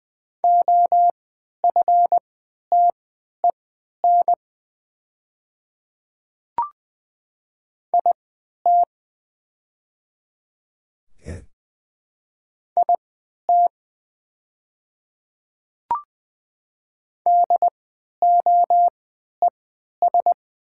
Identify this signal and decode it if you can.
Morse code, heard as a keyed tone, about 20 wpm, spaced as for 11 wpm, 700 Hz. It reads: OFTEN IT IT DOES
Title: 3x Character Spacing - Top 300 Words - 20wpm